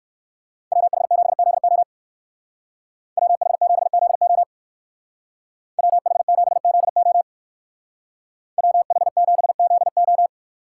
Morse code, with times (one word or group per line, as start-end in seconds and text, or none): 0.7-1.8 WH7ZO
3.2-4.4 WH7ZO
5.8-7.2 WH7ZO
8.6-10.3 WH7ZO